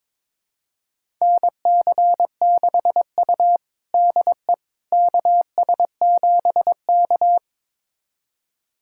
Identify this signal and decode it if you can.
Morse code, heard as a keyed tone, about 22 wpm, 700 Hz. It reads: NC6U DE KS7K
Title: USA Callsign DE Callsign - 4 Characters - 22wpm